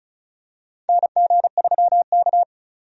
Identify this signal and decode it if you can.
Morse code, heard as a keyed tone, about 36 wpm, 690 Hz. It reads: NG3K
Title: USA Call Signs - 4 Characters - 35wpm